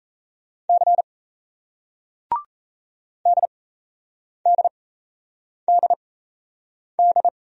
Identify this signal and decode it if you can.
Morse code, heard as a keyed tone, about 38 wpm, spaced as for 8 wpm, 700 Hz. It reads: C DDDD